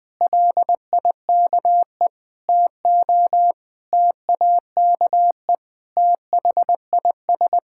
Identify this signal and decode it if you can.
Morse code, heard as a keyed tone, about 19 wpm, 700 Hz. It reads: LIKE TO TAKE THIS
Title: Sentences with 8-Word Phrases - 20wpm